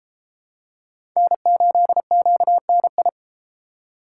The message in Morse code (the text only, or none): N8QNI